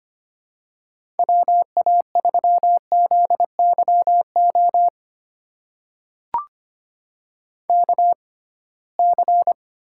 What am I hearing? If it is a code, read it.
Morse code, heard as a keyed tone, about 25 wpm, spaced as for 19 wpm, 700 Hz. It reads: WA3ZYO KC
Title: USA Call Signs - 6 Characters - 6x Character Spacing - 25wpm